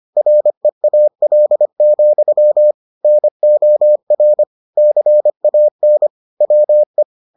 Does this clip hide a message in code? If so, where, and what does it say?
Yes: Morse code from 0.2 to 7.0 s, heard as a keyed tone, about 25 wpm, 600 Hz. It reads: REAL, NOR CAN WE